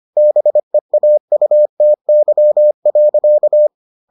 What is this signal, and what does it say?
Morse code, heard as a keyed tone, about 25 wpm, 600 Hz. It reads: BEAUTY.